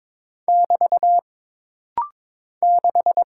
Morse code, heard as a keyed tone, about 22 wpm, 700 Hz. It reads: <BT> 6
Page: This rapid-fire timing is geared to push your ICR (Instant Character Recognition) skills.